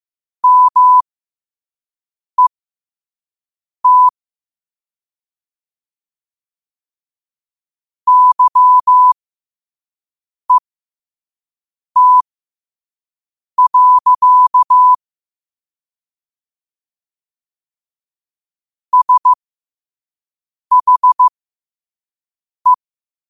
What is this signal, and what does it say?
Morse code, heard as a keyed tone, about 15 wpm, spaced as for 5 wpm, 1000 Hz. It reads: MET YET. SHE